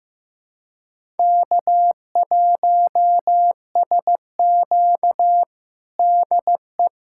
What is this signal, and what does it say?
Morse code, heard as a keyed tone, about 15 wpm, 700 Hz. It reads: K1SQ DE
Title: USA Callsign DE Callsign - 4 Characters - 15wpm